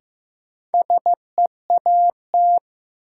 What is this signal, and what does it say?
Morse code, heard as a keyed tone, about 16 wpm, 700 Hz. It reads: SEAT
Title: Sentences from Top 1500 Words - New Word Drill Alt - 15wpm